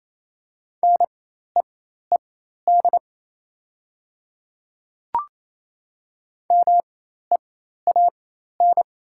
Morse code, heard as a keyed tone, about 27 wpm, spaced as for 13 wpm, 700 Hz. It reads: NEED MEAN
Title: 4x Character Spacing - Top 200 Words - 28wpm